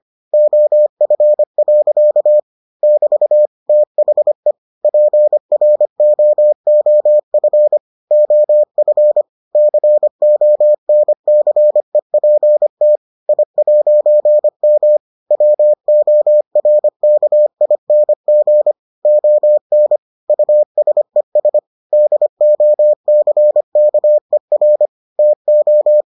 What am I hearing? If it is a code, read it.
Morse code, heard as a keyed tone, about 25 wpm, 600 Hz. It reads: OF. <BT> THE PROOF OF CONCEPT I'M WORKING ON USES DOCKER TO